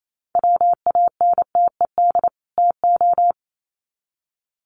Morse code, heard as a keyed tone, about 28 wpm, 700 Hz. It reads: WANTED TO
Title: Sets of 2 Words 28wpm